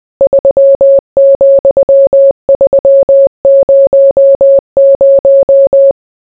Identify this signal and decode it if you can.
Morse code, heard as a keyed tone, about 20 wpm, 550 Hz. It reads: 3,300